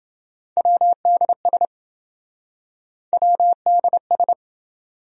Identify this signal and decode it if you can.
Morse code, heard as a keyed tone, about 27 wpm, 700 Hz. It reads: WDS WDS